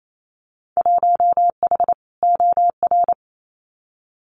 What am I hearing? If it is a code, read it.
Morse code, heard as a keyed tone, about 28 wpm, 700 Hz. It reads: 1H OR
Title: Winter Field Day Exchanges 28wpm